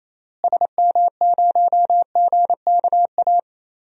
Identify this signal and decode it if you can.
Morse code, heard as a keyed tone, about 28 wpm, 700 Hz. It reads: SM0GKA
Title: DXCC-100 Call Signs - 6 Characters - Repeated 3x - 28wpm